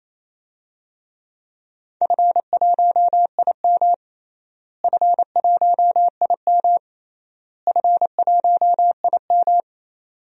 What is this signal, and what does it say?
Morse code, heard as a keyed tone, about 27 wpm, 700 Hz. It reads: F1IM F1IM F1IM
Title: DXCC-100 Call Signs - 4 Characters - Repeated 3x - 28wpm